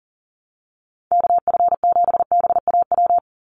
Morse code, heard as a keyed tone, about 40 wpm, 700 Hz. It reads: KF7BAW
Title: USA Call Signs 40wpm